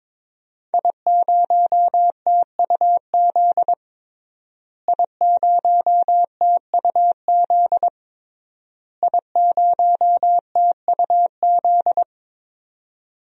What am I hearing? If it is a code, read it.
Morse code, heard as a keyed tone, about 22 wpm, 700 Hz. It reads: I0TUZ I0TUZ I0TUZ
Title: DXCC-100 Call Signs - 5 Characters - Repeated 3x - 22wpm